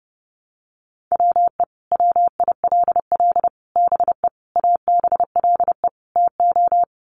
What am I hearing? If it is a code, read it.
Morse code, heard as a keyed tone, about 30 wpm, 700 Hz. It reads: WE WILL BE ABLE TO
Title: Sets of 5 Words Encore 30wpm